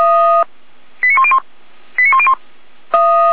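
Five-tone selective call (ZVEI), heard as quick runs of five tones, tones of about 70 ms. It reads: 89101 89101
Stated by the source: ZVEI five-tone selective call received on an SDR, logged on the Feuerwehr Orlamünde fire brigade audio server.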